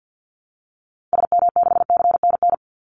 Morse code, heard as a keyed tone, about 48 wpm, 700 Hz. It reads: SM6CNN
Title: Call Signs Repeated 2x - 50wpm